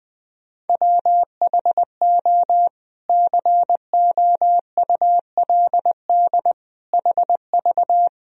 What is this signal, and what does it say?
Morse code, heard as a keyed tone, about 21 wpm, 700 Hz. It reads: WHO COULD HV
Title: Sentences from Top 300 Words and Abbreviations 20wpm